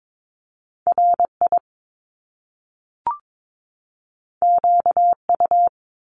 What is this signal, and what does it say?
Morse code, heard as a keyed tone, about 22 wpm, 700 Hz. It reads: RI QU
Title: Top Two Letter Combinations in Words 22wpm